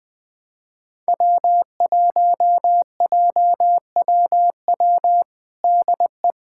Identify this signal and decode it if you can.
Morse code, heard as a keyed tone, about 20 wpm, 700 Hz. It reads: W1JWW DE
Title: USA Callsign DE Callsign 20wpm